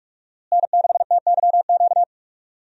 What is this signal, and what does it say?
Morse code, heard as a keyed tone, about 47 wpm, 700 Hz. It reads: N6TYQ
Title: USA Call Signs - 5 Characters - Repeated 2x - 45wpm